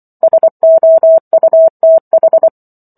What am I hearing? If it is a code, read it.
Morse code, heard as a keyed tone, about 24 wpm, 650 Hz. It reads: SOUTH